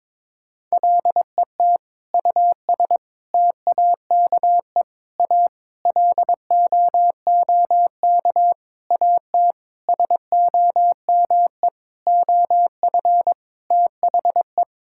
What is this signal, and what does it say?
Morse code, heard as a keyed tone, about 22 wpm, 700 Hz. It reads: LET US TAKE A LOOK AT SOME OF THE
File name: practice / 22wpm / Sentences with 8-Word Phrases - 22wpm